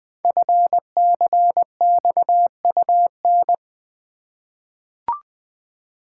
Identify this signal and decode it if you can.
Morse code, heard as a keyed tone, about 20 wpm, 700 Hz. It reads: FCXUN
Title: Code Groups - 5 Letters - 20wpm